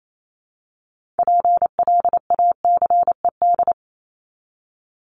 Morse code, heard as a keyed tone, about 28 wpm, 700 Hz. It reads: PLACED
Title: ED Suffix - Top 100 Words with Base Words 28wpm